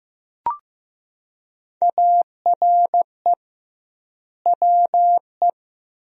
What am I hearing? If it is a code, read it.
Morse code, heard as a keyed tone, about 15 wpm, 700 Hz. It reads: ARE WE
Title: 2x Word-Spacing - Sentences from Top 300 Words - 15wpm